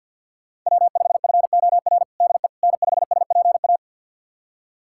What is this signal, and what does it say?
Morse code, heard as a keyed tone, about 52 wpm, 700 Hz. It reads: W5FOR DE N5SPA
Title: USA Callsign DE Callsign - 5 Characters - 50wpm